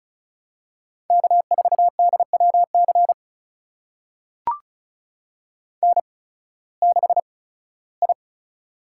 Morse code, heard as a keyed tone, about 36 wpm, spaced as for 19 wpm, 700 Hz. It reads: K4DWC N6I
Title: USA Call Signs - 5 Characters - 8x Character Spacing - 35wpm